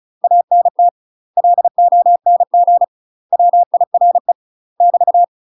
Morse code, heard as a keyed tone, about 36 wpm, 700 Hz. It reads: ANT LONG WIRE <BT>
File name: UK MASTER QSOs TRAINING/35WPM